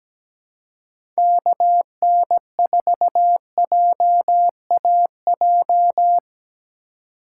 Morse code, heard as a keyed tone, about 17 wpm, 700 Hz. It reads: KN4JAJ
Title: USA Call Signs - 6 Characters - 2x Character Spacing - 17wpm